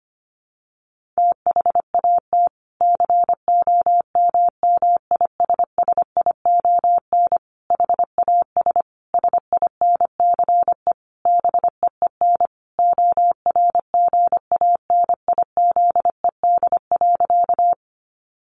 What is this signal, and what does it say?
Morse code, heard as a keyed tone, about 25 wpm, 700 Hz. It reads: THAT COMMISSION HAS SINCE BEEN ORGANIZED.
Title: Sentences from Top 1400 Words 25wpm